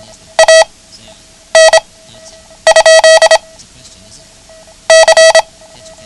Morse code, heard as a keyed tone, about 26 wpm, 650 Hz. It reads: A N ? C
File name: ham radio / CodeQuick / Code Quick 09